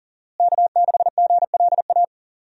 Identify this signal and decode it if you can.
Morse code, heard as a keyed tone, about 40 wpm, 700 Hz. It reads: K6GLA
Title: USA Call Signs - 5 Characters - Repeated 3x - 40wpm